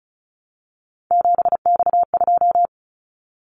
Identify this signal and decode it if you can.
Morse code, heard as a keyed tone, about 36 wpm, 700 Hz. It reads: 7X2
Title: Callsign Prefixes for Top 100 DXCC - 35wpm